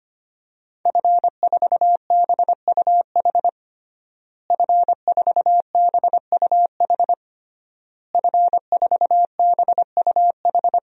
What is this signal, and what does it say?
Morse code, heard as a keyed tone, about 25 wpm, 700 Hz. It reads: F4BUH F4BUH F4BUH
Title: DXCC-100 Call Signs - 5 Characters - Repeated 3x - 25wpm